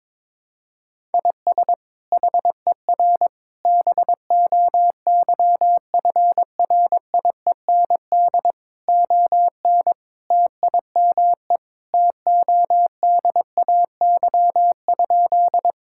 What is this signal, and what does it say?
Morse code, heard as a keyed tone, about 22 wpm, 700 Hz. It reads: IS HER BOYFRIEND ON TIME TODAY?
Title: Sentences from Top 1000 Words - Review Part 5 - 22wpm